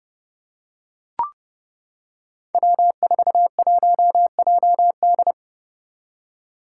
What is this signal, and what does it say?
Morse code, heard as a keyed tone, about 32 wpm, 700 Hz. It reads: W41JD
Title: Five Challenging Letter-Number Combinations 30wpm